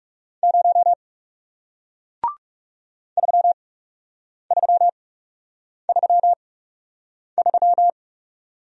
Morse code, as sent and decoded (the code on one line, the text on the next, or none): ----- / ...-- ...-- ...-- ...--
0 3333